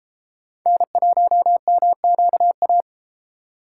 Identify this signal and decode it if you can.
Morse code, heard as a keyed tone, about 30 wpm, 700 Hz. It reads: N1MQA